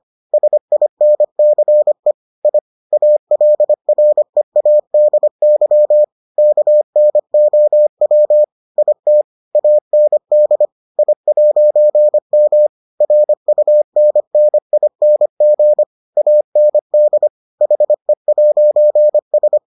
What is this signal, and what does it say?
Morse code, heard as a keyed tone, about 25 wpm, 600 Hz. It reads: SINCE I ALREADY KNOW IT AND I'M RUNNING AND HE'S